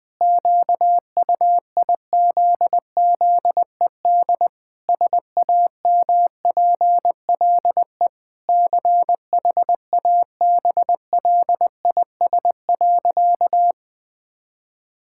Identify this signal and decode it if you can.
Morse code, heard as a keyed tone, about 21 wpm, 700 Hz. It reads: QUIZZED SAMPLE CHABLIS.
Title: Pangram Challenge v2 - 20wpm